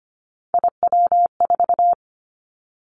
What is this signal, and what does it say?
Morse code, heard as a keyed tone, about 25 wpm, 700 Hz. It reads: IW4
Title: Callsign Prefixes for Top 25 DXCC - Repeated 3x - 25wpm